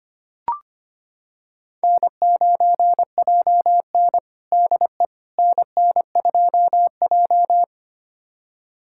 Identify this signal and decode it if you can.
Morse code, heard as a keyed tone, about 25 wpm, 700 Hz. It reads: N9JN DE NN2J
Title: USA Callsign DE Callsign - 4 Characters - 25wpm